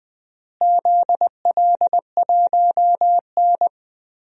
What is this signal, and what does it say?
Morse code, heard as a keyed tone, about 20 wpm, 700 Hz. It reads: ZL1N
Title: DXCC-100 Call Signs - 4 Characters - Repeated 2x - 20wpm